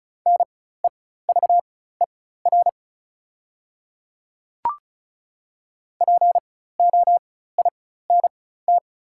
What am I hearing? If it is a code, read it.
Morse code, heard as a keyed tone, about 34 wpm, spaced as for 16 wpm, 700 Hz. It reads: NEVER POINT
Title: 4x Character Spacing - Top 300 Words - 35wpm